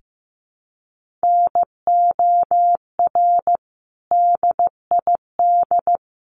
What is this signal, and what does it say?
Morse code, heard as a keyed tone, about 15 wpm, 700 Hz. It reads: NOR DID
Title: Sentences from Top 1500 Words 15wpm